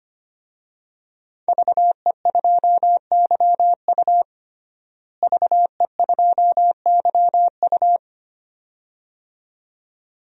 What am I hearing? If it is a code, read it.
Morse code, heard as a keyed tone, about 25 wpm, 700 Hz. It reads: VE2YU VE2YU
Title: DXCC-100 Call Signs - 5 Characters - Repeated 2x - 25wpm